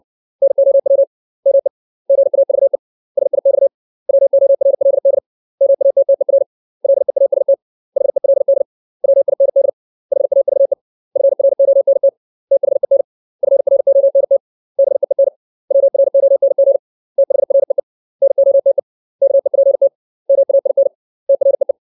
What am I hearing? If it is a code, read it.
Morse code, heard as a keyed tone, about 60 wpm, 550 Hz. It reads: NOW ME GAVE SIX WORLD MATTER LEAST HELD WIND HAVE FRONT THEN FRONT BEEN WRONG THREE NOTE KEPT MAIN TREE
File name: 632 words / 632 Words 60WPM